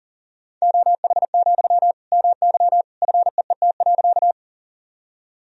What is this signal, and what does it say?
Morse code, heard as a keyed tone, about 40 wpm, 700 Hz. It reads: OH, MY FEET.
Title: Sentences from Top 1000 Words - Review Part 1 - 40wpm